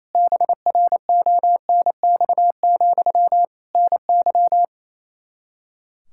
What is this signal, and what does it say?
Morse code, heard as a keyed tone, about 27 wpm, 700 Hz. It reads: BRONX, NY
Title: Practice QSO - Exchange 1 - Beginning - 28wpm